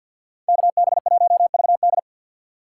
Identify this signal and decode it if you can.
Morse code, heard as a keyed tone, about 60 wpm, 700 Hz. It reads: KB1VD
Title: Speed Racing - USA 2x2 Call Signs 50wpm